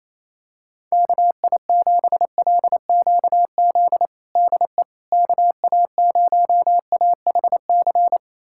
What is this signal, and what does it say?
Morse code, heard as a keyed tone, about 28 wpm, 700 Hz. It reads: KI7LQZ DE KA0AHC
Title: USA Callsign DE Callsign 28wpm